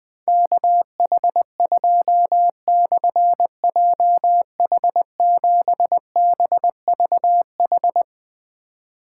Morse code, heard as a keyed tone, about 20 wpm, 700 Hz. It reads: KH2/JH7BVH
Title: Difficult Call Signs 20wpm